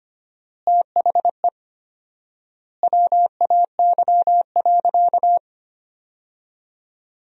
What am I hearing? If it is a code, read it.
Morse code, heard as a keyed tone, about 25 wpm, 700 Hz. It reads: THE WAY.
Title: 4x Word-Spacing - Sentences from Top 200 Words - 25wpm